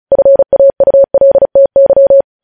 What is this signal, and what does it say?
Morse code, heard as a keyed tone, about 36 wpm, 550 Hz. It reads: FAULTY
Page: News headlines in Morse code at 35 wpm